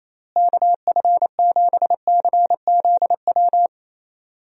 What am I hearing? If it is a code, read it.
Morse code, heard as a keyed tone, about 28 wpm, 700 Hz. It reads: KF7CZW
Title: USA Call Signs - 6 Characters - Repeated 3x - 28wpm